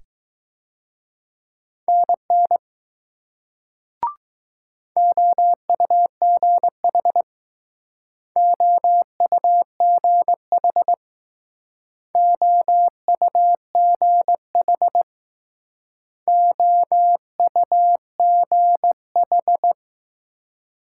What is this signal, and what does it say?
Morse code, heard as a keyed tone, about 20 wpm, 700 Hz. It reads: NN OUGH OUGH OUGH OUGH